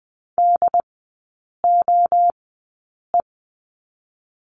Morse code, heard as a keyed tone, about 20 wpm, spaced as for 8 wpm, 700 Hz. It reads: DOE
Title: 133 Lesson 22e - Question Mark - Three Characters 20wpm